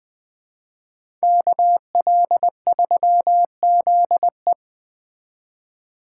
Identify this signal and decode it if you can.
Morse code, heard as a keyed tone, about 20 wpm, 700 Hz. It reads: KL3ZE
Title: Speed Racing - USA 2x2 Call Signs 17wpm